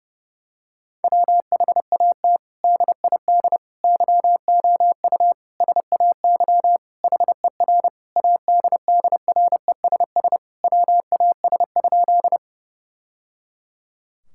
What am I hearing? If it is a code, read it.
Morse code, heard as a keyed tone, about 30 wpm, 700 Hz. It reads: WHAT DID YOU SAY HER ADDRESS WAS?
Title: Sentences from Top 1000 Words - Review Part 6 - 30wpm